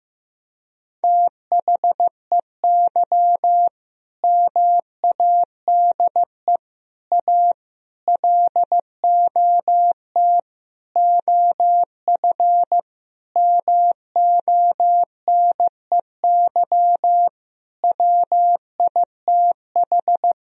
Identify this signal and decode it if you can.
Morse code, heard as a keyed tone, about 15 wpm, 700 Hz. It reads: THEY MADE A LOT OF MONEY WITH
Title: Sentences from Top 1000 Words - Review Part 4 - 15wpm